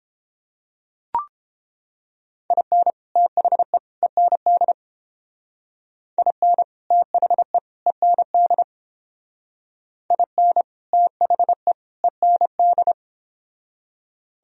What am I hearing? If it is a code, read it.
Morse code, heard as a keyed tone, about 28 wpm, 700 Hz. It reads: IN THE END IN THE END IN THE END